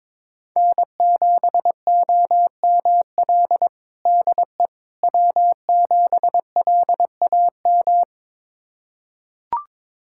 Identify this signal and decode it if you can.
Morse code, heard as a keyed tone, about 22 wpm, 700 Hz. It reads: N7OML DE W7LAM
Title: USA Callsign DE Callsign - 5 Characters - 22wpm